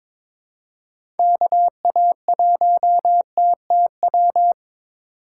Morse code, heard as a keyed tone, about 22 wpm, 700 Hz. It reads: KA1TTW
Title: USA Call Signs - 6 Characters - 2x Character Spacing - 22wpm